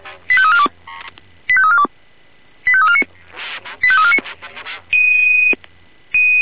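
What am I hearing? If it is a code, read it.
Five-tone selective call (ZVEI), heard as quick runs of five tones, tones of about 70 ms. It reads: 85342 85342 85349 85349
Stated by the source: ZVEI five-tone selective call received on SDR.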